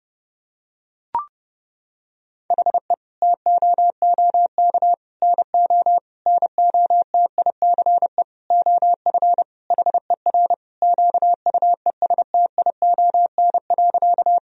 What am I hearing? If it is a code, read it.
Morse code, heard as a keyed tone, about 30 wpm, 700 Hz. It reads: HE TOOK NO NOTICE OF HER QUESTION.
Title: Sentences from Top 1000 Words - Review Part 2 - 30wpm